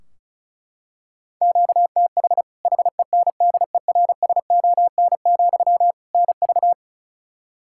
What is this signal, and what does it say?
Morse code, heard as a keyed tone, about 34 wpm, 700 Hz. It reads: QTH HENDERSON, NV
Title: Practice QSO - Exchange 1 - Beginning - 35wpm